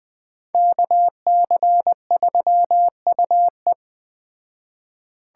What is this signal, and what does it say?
Morse code, heard as a keyed tone, about 21 wpm, 700 Hz. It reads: KC3UE
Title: Speed Racing - USA Call Signs 17wpm